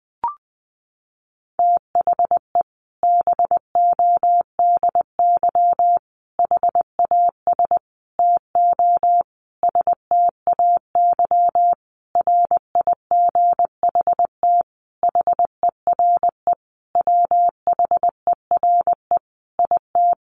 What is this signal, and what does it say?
Morse code, heard as a keyed tone, about 20 wpm, 700 Hz. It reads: THE BODY HAS TO STAY RIGHT HERE WHERE IT
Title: A Sound Of Thunder 20wpm